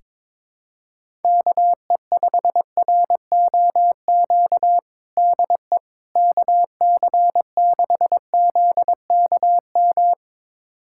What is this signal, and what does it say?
Morse code, heard as a keyed tone, about 22 wpm, 700 Hz. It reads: KE5ROQ DE KC6ZKM
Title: USA Callsign DE Callsign - 6 Characters - 22wpm